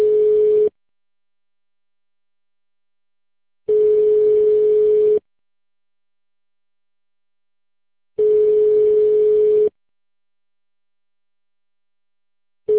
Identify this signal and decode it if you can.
Ringback tone (425 Hz, 1.5 s on, 3 s off).